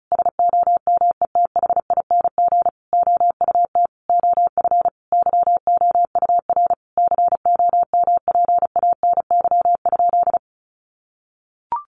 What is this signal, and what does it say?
Morse code, heard as a keyed tone, about 34 wpm, 700 Hz. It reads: SOMETHING OUT OF YOUR COMPANY?
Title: Sentences from Top 600 Words 35wpm